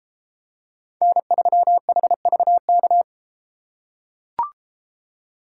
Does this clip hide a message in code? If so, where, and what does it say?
Yes: Morse code from 1.0 to 3.0 s, heard as a keyed tone, about 32 wpm, 700 Hz. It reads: N3HVK